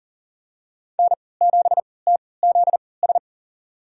Morse code, heard as a keyed tone, about 40 wpm, spaced as for 23 wpm, 700 Hz. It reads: N7TZS
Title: USA Call Signs - 5 Characters - 3x Character Spacing - 40wpm